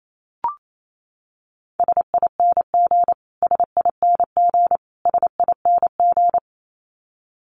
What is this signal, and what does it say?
Morse code, heard as a keyed tone, about 28 wpm, 700 Hz. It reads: SING SING SING
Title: Top 500 Words Repeated 28wpm